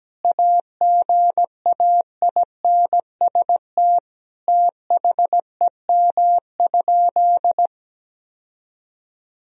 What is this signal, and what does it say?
Morse code, heard as a keyed tone, about 17 wpm, 700 Hz. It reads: AGAINST THEM?
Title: Sentences from Top 1000 Words - Review Part 4 - 17wpm